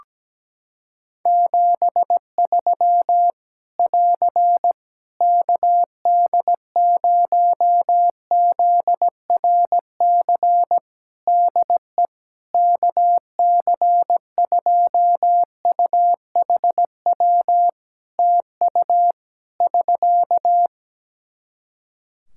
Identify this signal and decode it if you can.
Morse code, heard as a keyed tone, about 17 wpm, 700 Hz. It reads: 73 <AR> KD0ZRC DE KC2UHW TU <SK>
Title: Practice QSO - Exchange 4 - Ending - 17wpm